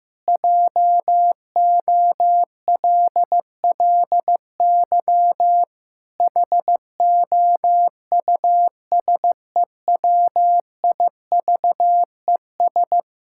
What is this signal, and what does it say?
Morse code, heard as a keyed tone, about 14 wpm, 700 Hz. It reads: JOLLY HOUSEWIVES
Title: Pangram Challenge v2 - 15wpm